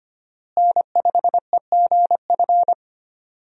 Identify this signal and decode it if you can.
Morse code, heard as a keyed tone, about 25 wpm, 700 Hz. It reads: N5EGF
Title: USA Call Signs - 5 Characters - Repeated 2x - 25wpm